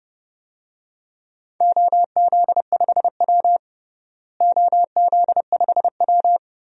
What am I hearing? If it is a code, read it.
Morse code, heard as a keyed tone, about 32 wpm, 700 Hz. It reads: OZ5W OZ5W
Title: DXCC-100 Call Signs - 4 Characters - Repeated 3x - 30wpm